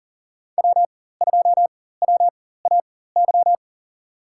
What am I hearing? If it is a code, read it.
Morse code, heard as a keyed tone, about 40 wpm, spaced as for 19 wpm, 700 Hz. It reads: W2WAY
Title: USA Call Signs - 4x Character Spacing - 40wpm